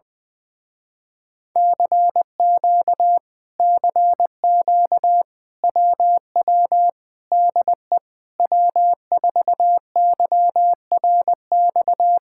Morse code, heard as a keyed tone, about 20 wpm, 700 Hz. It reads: CQ CQ WW DE W4YRX
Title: Speed-Racing Common QSO Phrases - 20wpm